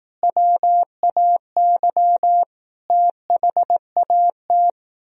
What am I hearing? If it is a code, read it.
Morse code, heard as a keyed tone, about 18 wpm, 700 Hz. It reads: WAY THAT
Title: Speed-Racing - Sets of 5 Words 15wpm